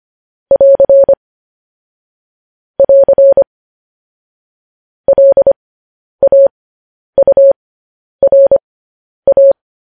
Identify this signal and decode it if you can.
Morse code, heard as a keyed tone, about 25 wpm, spaced as for 10 wpm, 550 Hz. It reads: <AR> <AR> LAURA
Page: news headlines in morse code at 10 wpm